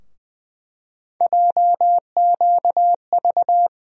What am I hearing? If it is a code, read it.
Morse code, heard as a keyed tone, about 20 wpm, 700 Hz. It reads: JQV
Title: USA Call Sign Postfixes - 3 Letters 20wpm